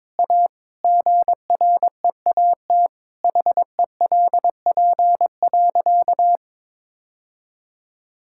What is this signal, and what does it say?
Morse code, heard as a keyed tone, about 22 wpm, 700 Hz. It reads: A GREAT HELP.
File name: practice / 22wpm / Sentences with 3-Word Phrases - 22wpm